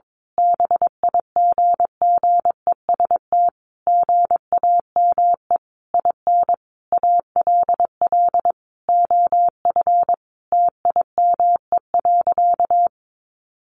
Morse code, heard as a keyed tone, about 22 wpm, 700 Hz. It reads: BIGGEST GAME IN ALL OF TIME.
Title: A Sound Of Thunder 22wpm